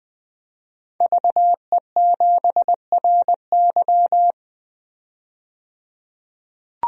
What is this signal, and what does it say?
Morse code, heard as a keyed tone, about 21 wpm, 700 Hz. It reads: VE7RY
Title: DXCC-100 Call Signs - 5 Characters - 20wpm